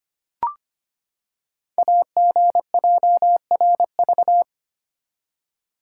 Morse code, heard as a keyed tone, about 25 wpm, 700 Hz. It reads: AGJRV
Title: Code Groups - 5 Letter:Numbers - 25wpm